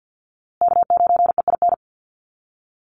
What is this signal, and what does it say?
Morse code, heard as a keyed tone, about 50 wpm, 700 Hz. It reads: K9EIN
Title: USA Call Signs Repeated 2x 50wpm